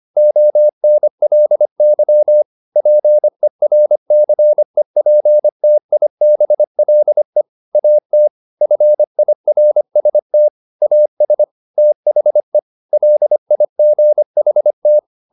Morse code, heard as a keyed tone, about 25 wpm, 600 Hz. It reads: ONLY PERCEPTIBLE AT FIRST AS THE LIGHT